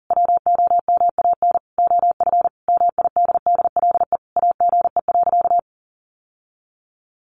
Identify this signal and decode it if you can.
Morse code, heard as a keyed tone, about 40 wpm, 700 Hz. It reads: WOMAN OF MIDDLE AGE.
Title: Sentences from Top 1400 Words 40wpm